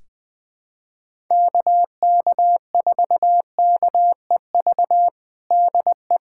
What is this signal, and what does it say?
Morse code, heard as a keyed tone, about 20 wpm, 700 Hz. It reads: KK4KEV DE
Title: USA Callsign DE Callsign - 6 Characters - 20wpm